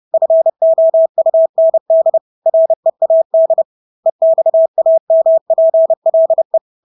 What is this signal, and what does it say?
Morse code, heard as a keyed tone, about 30 wpm, 650 Hz. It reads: FOUND READ EXAMPLE
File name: Engelske ord 30-30 wpm